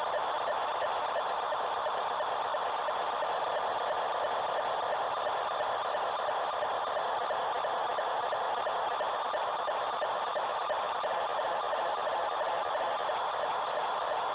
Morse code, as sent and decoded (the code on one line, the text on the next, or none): none